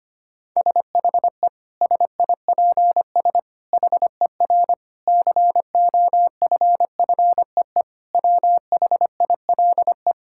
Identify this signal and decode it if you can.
Morse code, heard as a keyed tone, about 25 wpm, 700 Hz. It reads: SHE SIPS HER COFFEE WHILE
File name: practice / 25wpm / Difficult Sentences with Hints - 25wpm